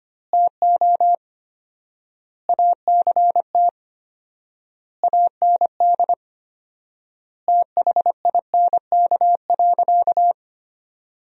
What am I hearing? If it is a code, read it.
Morse code, heard as a keyed tone, about 25 wpm, 700 Hz. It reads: TO ACT AND THINK.
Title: 4x Word-Spacing - Sentences from Top 500 Words - 25wpm